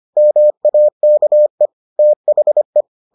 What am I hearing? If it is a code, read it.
Morse code, heard as a keyed tone, about 25 wpm, 600 Hz. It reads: MAKE THE